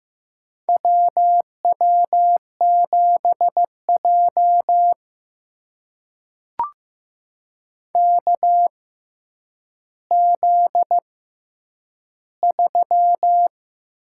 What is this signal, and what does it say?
Morse code, heard as a keyed tone, about 15 wpm, spaced as for 9 wpm, 700 Hz. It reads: WW7J KZ3
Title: USA Call Signs - 4 Characters - 6x Character Spacing - 15wpm